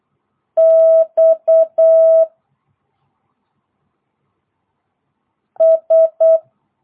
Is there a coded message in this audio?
Yes: Morse code from 0.6 to 6.4 s, heard as a keyed tone, about 8 wpm, 650 Hz. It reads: X S